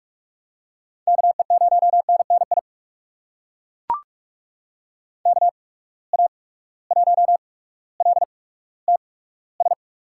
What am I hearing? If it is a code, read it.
Morse code, heard as a keyed tone, about 41 wpm, spaced as for 19 wpm, 700 Hz. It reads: KE0NNI KA1LTS
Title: USA Call Signs - 6 Characters - 8x Character Spacing - 45wpm